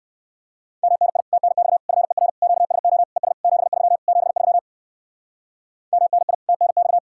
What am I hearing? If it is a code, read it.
Morse code, heard as a keyed tone, about 59 wpm, 700 Hz. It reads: MNI TNX FER QSO ES 73 73 MNI TNX